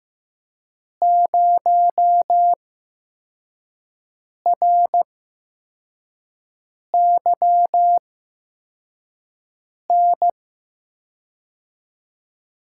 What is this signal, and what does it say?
Morse code, heard as a keyed tone, about 15 wpm, spaced as for 4 wpm, 700 Hz. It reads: 0RYN
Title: USA Call Signs - 5 Characters - 8x Character Spacing - 15wpm